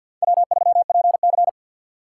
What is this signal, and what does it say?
Morse code, heard as a keyed tone, about 52 wpm, 700 Hz. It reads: W3PC
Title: USA Call Signs - 4 Characters - 3x Character Spacing - 50wpm